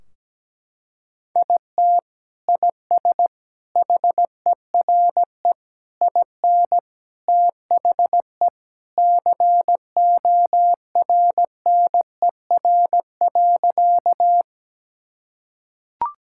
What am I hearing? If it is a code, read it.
Morse code, heard as a keyed tone, about 17 wpm, 700 Hz. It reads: IT IS HERE IN THE CORNER.
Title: Sentences from Top 1600 Words 17wpm